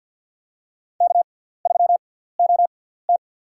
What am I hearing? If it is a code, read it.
Morse code, heard as a keyed tone, about 47 wpm, spaced as for 17 wpm, 700 Hz. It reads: K3OT
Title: USA Call Signs - 4 Characters - 6x Character Spacing - 50wpm